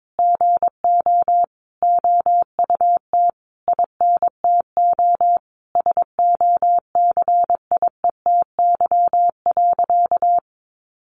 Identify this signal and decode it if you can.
Morse code, heard as a keyed tone, about 22 wpm, 700 Hz. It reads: GO OUT INTO SOCIETY.
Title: Sentences from Top 700 Words 22wpm